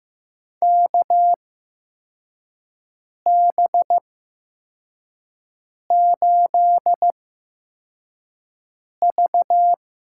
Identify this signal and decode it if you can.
Morse code, heard as a keyed tone, about 15 wpm, spaced as for 4 wpm, 700 Hz. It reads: KB8V
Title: USA Call Signs - 8x Character Spacing - 15wpm